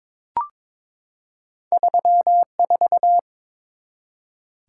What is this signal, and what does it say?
Morse code, heard as a keyed tone, about 22 wpm, 690 Hz. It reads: 34
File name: practice / 22wpm / Two Letter-Number Combinations - 22wpm